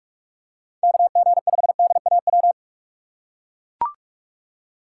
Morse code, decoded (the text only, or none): KG5DAW